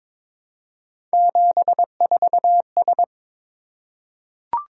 Morse code, heard as a keyed tone, about 22 wpm, 700 Hz. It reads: 74S
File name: practice / 22wpm / Three Letter-Number Combinations - 22wpm